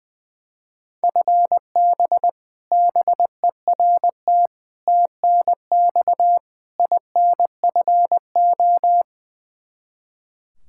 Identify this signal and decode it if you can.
Morse code, heard as a keyed tone, about 20 wpm, 700 Hz. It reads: FB BERT TNX INFO
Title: Practice QSO - Exchange 3 - More Info - 20wpm